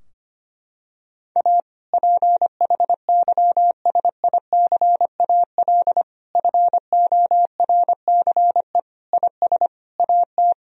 Morse code, heard as a keyed tone, about 25 wpm, 700 Hz. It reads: A PHYSICAL FORCE IS AT
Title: Sentences from Top 1000 Words - Review Part 4 - 25wpm